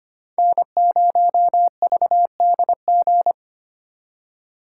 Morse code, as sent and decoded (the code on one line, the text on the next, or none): -. ----- ...- -.. --.
N0VDG